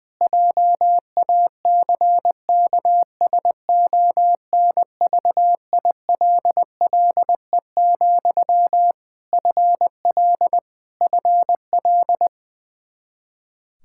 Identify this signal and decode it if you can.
Morse code, heard as a keyed tone, about 20 wpm, 700 Hz. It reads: JACKSONVILLE, FL FL
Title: Practice QSO - Exchange 1 - Beginning - 20wpm